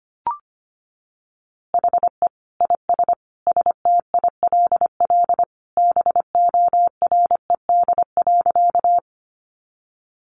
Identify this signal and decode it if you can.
Morse code, heard as a keyed tone, about 25 wpm, 700 Hz. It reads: HE IS STILL BORED.
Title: Easy Phrases 25wpm